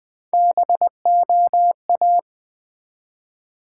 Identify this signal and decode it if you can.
Morse code, heard as a keyed tone, about 20 wpm, 700 Hz. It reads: BOA
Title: Build A Word - 1 Letter At A Time - 3 to 5-Letter Words - 20wpm